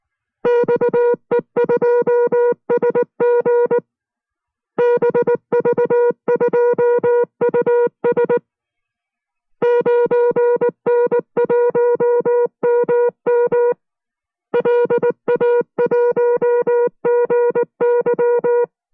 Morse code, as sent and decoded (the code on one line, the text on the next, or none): -..- . ..--- ... --. / -... ...- ..--- ..- ... / ----. -. .---- -- -- / .-.. .- .---- --. -.--
XE2SG BV2US 9N1MM LA1GY